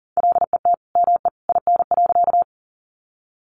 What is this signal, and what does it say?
Morse code, heard as a keyed tone, about 40 wpm, 700 Hz. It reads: LET ME IN.
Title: Sentences from Top 300 Words 40wpm